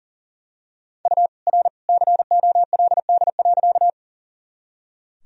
Morse code, heard as a keyed tone, about 39 wpm, 700 Hz. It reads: U R COLD.
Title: Sentences from Top 400 Words and Abbreviations 40wpm